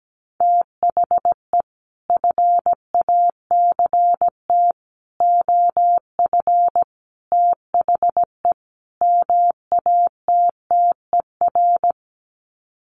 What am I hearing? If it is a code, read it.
Morse code, heard as a keyed tone, about 17 wpm, 700 Hz. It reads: THE FACT OF THE MATTER
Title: Sets of 5 Words 17wpm